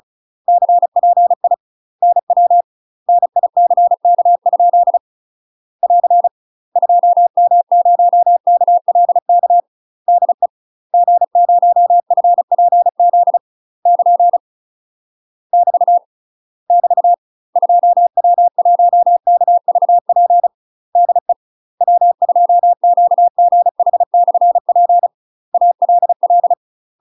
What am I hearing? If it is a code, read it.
Morse code, heard as a keyed tone, about 36 wpm, 700 Hz. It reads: CPI NW NICK? <AR> 2M0KLK DE G0FPZ <KN> <BT> <BT> 2W1KVP DE W2QGH/P ALL
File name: UK MASTER QSOs TRAINING / 35WPM / UK MASTER QSOs 0031